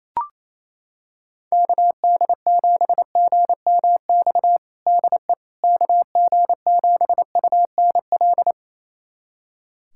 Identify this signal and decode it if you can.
Morse code, heard as a keyed tone, about 28 wpm, 700 Hz. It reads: KD7GMX DE KG7UNL
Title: USA Callsign DE Callsign - 6 Characters - 28wpm